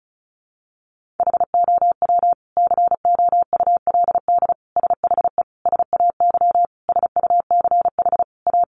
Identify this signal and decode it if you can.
Morse code, heard as a keyed tone, about 34 wpm, 700 Hz. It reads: HOW COULD SHE SAY SUCH A
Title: Sentences from Top 200 Words 35wpm